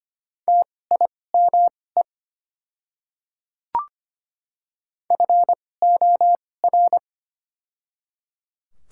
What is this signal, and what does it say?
Morse code, heard as a keyed tone, about 25 wpm, spaced as for 18 wpm, 700 Hz. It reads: TIME FOR
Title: 2x Character Spacing - Top 100 Words - 25wpm